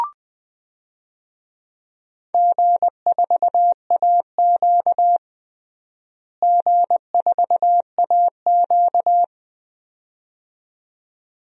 Morse code, heard as a keyed tone, about 20 wpm, 700 Hz. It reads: G4AQ G4AQ
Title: DXCC-100 Call Signs - 4 Characters - Repeated 2x - 20wpm